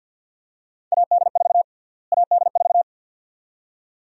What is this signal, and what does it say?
Morse code, heard as a keyed tone, about 49 wpm, 700 Hz. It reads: AD4 AD4
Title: USA Call Sign Prefixes - Repeated 2x - 50wpm